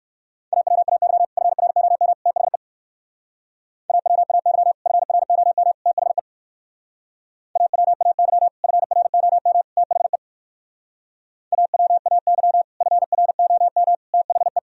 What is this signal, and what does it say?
Morse code, heard as a keyed tone, about 56 wpm, 700 Hz. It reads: AWAY FROM THE AWAY FROM THE AWAY FROM THE AWAY FROM THE